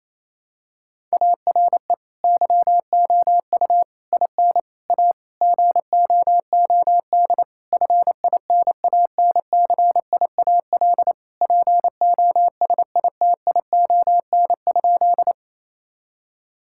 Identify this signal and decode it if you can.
Morse code, heard as a keyed tone, about 28 wpm, 700 Hz. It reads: ARE YOU IN A GOOD FINANCIAL POSITION?
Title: Sentences from Top 1000 Words - Review Part 5 - 28wpm